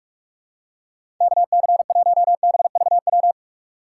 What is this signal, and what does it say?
Morse code, heard as a keyed tone, about 41 wpm, 700 Hz. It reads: KC1BVW
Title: USA Call Signs - 6 Characters - 6x Character Spacing - 45wpm